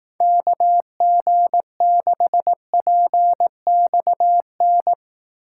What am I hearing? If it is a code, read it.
Morse code, heard as a keyed tone, about 18 wpm, 700 Hz. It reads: KG6PXN
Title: Speed Racing - USA 2x3 Call Signs 15wpm